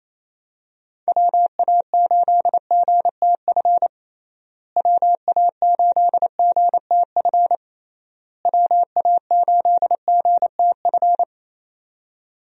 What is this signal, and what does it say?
Morse code, heard as a keyed tone, about 28 wpm, 700 Hz. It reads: WA8GTF WA8GTF WA8GTF
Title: USA Call Signs - 6 Characters - Repeated 3x - 28wpm